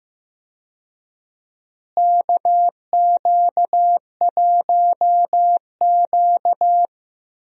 Morse code, heard as a keyed tone, about 15 wpm, 700 Hz. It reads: KQ1Q
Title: USA Call Signs - 4 Characters - Repeated 3x - 15wpm